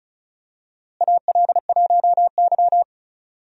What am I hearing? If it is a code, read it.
Morse code, heard as a keyed tone, about 36 wpm, 700 Hz. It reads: AL1Y
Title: USA Call Signs - 4 Characters - Repeated 3x - 35wpm